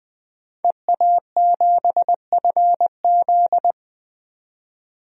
Morse code, heard as a keyed tone, about 21 wpm, 700 Hz. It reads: EA7FZ
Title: DXCC-100 Call Signs - 5 Characters - 20wpm